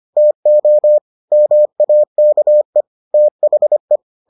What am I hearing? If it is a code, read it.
Morse code, heard as a keyed tone, about 25 wpm, 600 Hz. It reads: TO MAKE THE